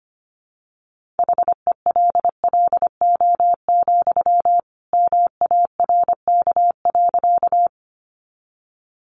Morse code, heard as a keyed tone, about 25 wpm, 700 Hz. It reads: HELLO, MARK.
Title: Sentences from Top 1100 Words 25wpm